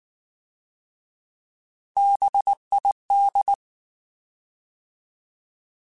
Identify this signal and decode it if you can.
Morse code, heard as a keyed tone, about 19 wpm, 800 Hz. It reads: BID
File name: Alphabetical / Alphabetical mp3s / Bs @18